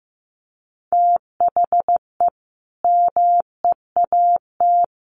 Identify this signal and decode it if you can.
Morse code, heard as a keyed tone, about 15 wpm, 700 Hz. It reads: THE MEAT
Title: A Sound Of Thunder 15wpm